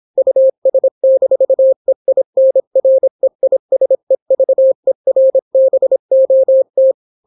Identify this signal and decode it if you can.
Morse code, heard as a keyed tone, about 26 wpm, 530 Hz. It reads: US-EINREISEVERBOT